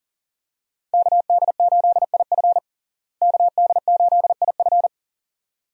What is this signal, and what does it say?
Morse code, heard as a keyed tone, about 40 wpm, 700 Hz. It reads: KD8IF KD8IF
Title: USA Call Signs - 5 Characters - Repeated 2x - 40wpm